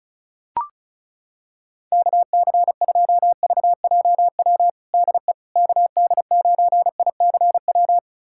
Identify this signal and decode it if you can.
Morse code, heard as a keyed tone, about 34 wpm, 690 Hz. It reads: KC2VJW DE KD9ICW
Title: USA Callsign DE Callsign 35wpm